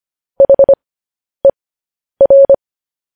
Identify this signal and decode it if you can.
Morse code, heard as a keyed tone, about 25 wpm, spaced as for 10 wpm, 550 Hz. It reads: HER
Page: news headlines in morse code at 10 wpm